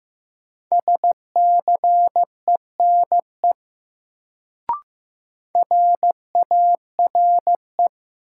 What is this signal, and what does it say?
Morse code, heard as a keyed tone, about 15 wpm, 700 Hz. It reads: SCENE RARE
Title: Sentences from Top 1100 Words - New Word Drill - 15wpm